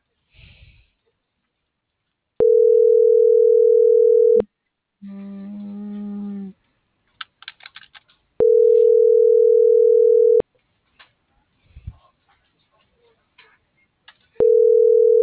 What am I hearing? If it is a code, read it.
Ringback tone (North America), 440 + 480 Hz.